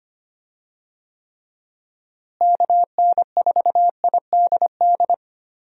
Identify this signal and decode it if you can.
Morse code, heard as a keyed tone, about 25 wpm, 700 Hz. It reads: KN4IDD
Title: USA Call Signs - 6 Characters - Repeated 3x - 25wpm